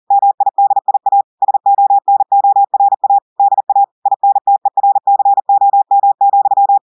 Morse code, heard as a keyed tone, about 40 wpm, 800 Hz. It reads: MIDIA SONORA DA INTERCOM,
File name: audios / portugais / portugais texte en clair 1 vitesse 40 mots minutes